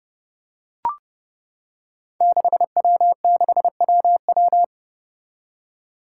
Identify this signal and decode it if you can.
Morse code, heard as a keyed tone, about 30 wpm, 700 Hz. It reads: 6W6WW
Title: Five Challenging Letter-Number Combinations 30wpm